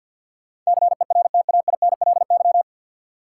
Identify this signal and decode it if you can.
Morse code, heard as a keyed tone, about 47 wpm, 700 Hz. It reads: CERTAINLY
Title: Sentences from Top 900 Words - New Word Drill - 50wpm